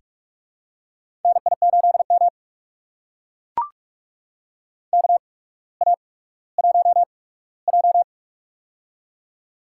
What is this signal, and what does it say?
Morse code, heard as a keyed tone, about 42 wpm, spaced as for 19 wpm, 700 Hz. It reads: NI8M KA1J
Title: USA Call Signs - 8x Character Spacing - 45wpm